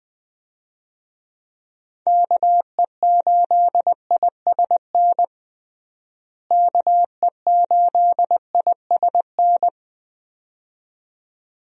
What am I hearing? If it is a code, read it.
Morse code, heard as a keyed tone, about 20 wpm, 700 Hz. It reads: KE8ISN KE8ISN
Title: USA Call Signs - 6 Characters - Repeated 2x - 20wpm